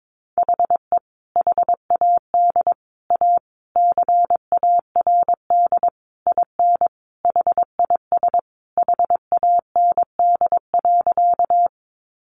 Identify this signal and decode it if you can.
Morse code, heard as a keyed tone, about 22 wpm, 700 Hz. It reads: HE HAD A CARD IN HIS HAND.
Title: Sentences from Top 600 Words 22wpm